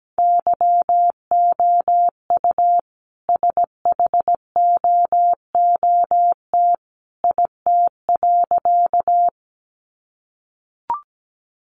Morse code, heard as a keyed tone, about 17 wpm, 700 Hz. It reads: YOU SHOOT IT.
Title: A Sound Of Thunder 17wpm